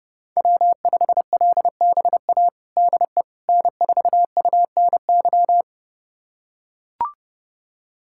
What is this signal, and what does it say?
Morse code, heard as a keyed tone, about 30 wpm, 700 Hz. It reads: W5LBA DE N4UNY
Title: USA Callsign DE Callsign - 5 Characters - 30wpm